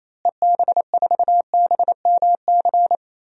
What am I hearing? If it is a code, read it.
Morse code, heard as a keyed tone, about 28 wpm, 700 Hz. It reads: EB4BMC
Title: DXCC-100 Call Signs - 6 Characters - Repeated 3x - 28wpm